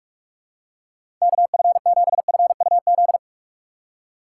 Morse code, heard as a keyed tone, about 43 wpm, 700 Hz. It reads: KF7FUZ
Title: USA Call Signs - 6 Characters - 4x Character Spacing - 45wpm